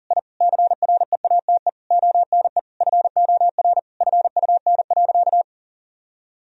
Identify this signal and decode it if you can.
Morse code, heard as a keyed tone, about 40 wpm, 700 Hz. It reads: I CREATE ONE FOR FUN.
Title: Sentences from Top 1000 Words - Review Part 3 - 40wpm